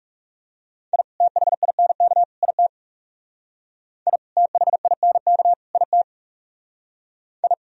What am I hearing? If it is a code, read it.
Morse code, heard as a keyed tone, about 38 wpm, 700 Hz. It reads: I THINK IT I THINK IT I